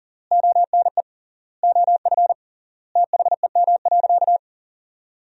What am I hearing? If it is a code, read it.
Morse code, heard as a keyed tone, about 37 wpm, 700 Hz. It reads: ONE OF THEM.
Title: 3x Word-Spacing - Sentences from Top 300 Words - 40wpm